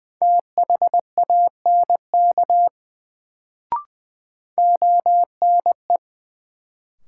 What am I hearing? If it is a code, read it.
Morse code, heard as a keyed tone, about 20 wpm, 700 Hz. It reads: THANK ONE
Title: Sets of 2 Words - New Word Drill - 20wpm